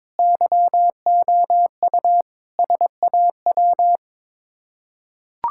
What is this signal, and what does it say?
Morse code, heard as a keyed tone, about 22 wpm, 700 Hz. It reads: YOU SAW
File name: practice / 22wpm / Sets of 7 Words - 22wpm